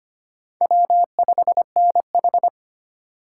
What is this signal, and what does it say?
Morse code, heard as a keyed tone, about 33 wpm, 700 Hz. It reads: W5NH